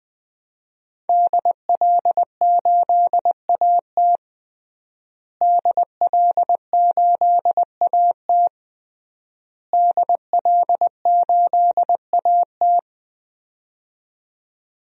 Morse code, heard as a keyed tone, about 20 wpm, 700 Hz. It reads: DL8AT DL8AT DL8AT
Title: DXCC-100 Call Signs - 5 Characters - Repeated 3x - 20wpm